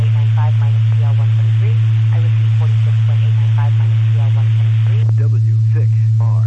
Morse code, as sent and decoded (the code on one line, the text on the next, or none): none